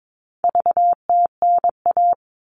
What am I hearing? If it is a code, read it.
Morse code, heard as a keyed tone, about 22 wpm, 700 Hz. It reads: VTNA
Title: All Cut Numbers - 4-Digit 22wpm